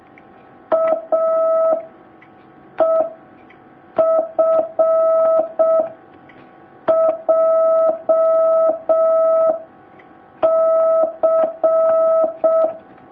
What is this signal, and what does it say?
Morse code, heard as a keyed tone, about 6 wpm, 650 Hz. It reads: A E F J C
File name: jh8cbh / 2019cw